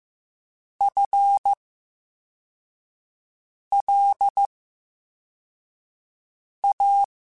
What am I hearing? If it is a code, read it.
Morse code, heard as a keyed tone, about 15 wpm, spaced as for 4 wpm, 800 Hz. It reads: FLA
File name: Special INGs / Special INGs-mp3s / Slow ing @13